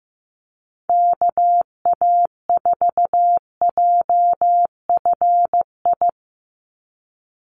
Morse code, heard as a keyed tone, about 15 wpm, 700 Hz. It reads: KA4JFI
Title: USA Call Signs 15wpm